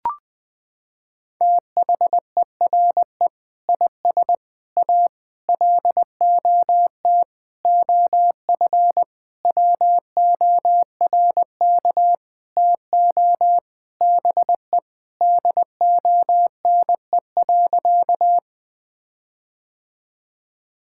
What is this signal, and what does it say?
Morse code, heard as a keyed tone, about 20 wpm, 700 Hz. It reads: THERE IS A LOT OF WORK TO BE DONE.
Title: Sentences with 7-Word Phrases - 20wpm